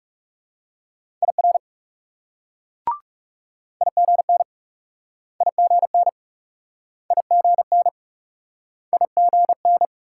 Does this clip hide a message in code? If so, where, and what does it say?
Yes: Morse code from 1.2 to 9.9 s, heard as a keyed tone, about 40 wpm, 700 Hz. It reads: IR IGN IGN IGN IGN